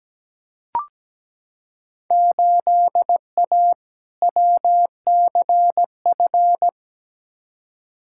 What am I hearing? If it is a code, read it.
Morse code, heard as a keyed tone, about 17 wpm, 700 Hz. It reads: 8A WCF
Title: Field Day Exchanges Encore - 17wpm